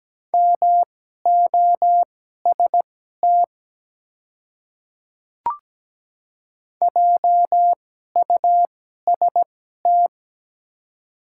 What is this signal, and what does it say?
Morse code, heard as a keyed tone, about 17 wpm, spaced as for 12 wpm, 700 Hz. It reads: MOST JUST
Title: 2x Character Spacing - Top 200 Words - 17wpm